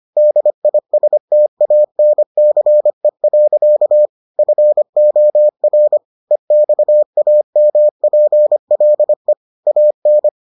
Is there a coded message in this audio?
Yes: Morse code from 0.2 to 10.3 s, heard as a keyed tone, about 25 wpm, 600 Hz. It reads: DISTANCE. FOR EXAMPLE AN